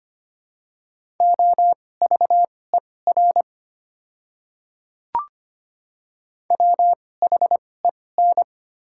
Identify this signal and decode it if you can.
Morse code, heard as a keyed tone, about 25 wpm, spaced as for 18 wpm, 700 Hz. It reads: OVER WHEN
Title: 2x Character Spacing - Top 100 Words - 25wpm